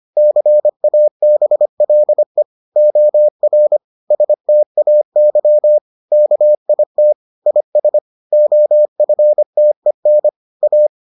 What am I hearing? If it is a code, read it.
Morse code, heard as a keyed tone, about 25 wpm, 600 Hz. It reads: CABLE OR STAY KIT IS OFTEN A